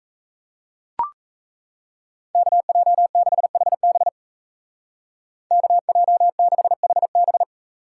Morse code, heard as a keyed tone, about 38 wpm, 700 Hz. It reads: KJ6HB KJ6HB